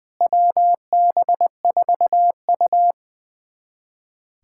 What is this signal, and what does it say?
Morse code, heard as a keyed tone, about 20 wpm, 690 Hz. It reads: WB4U
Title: USA Callsign DE Callsign 20wpm